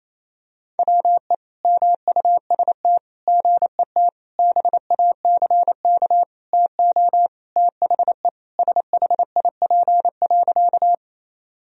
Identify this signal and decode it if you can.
Morse code, heard as a keyed tone, about 28 wpm, 700 Hz. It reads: WE MUST GET BACK TO THE SHIP.
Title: Sentences from Top 1000 Words - Review Part 2 - 28wpm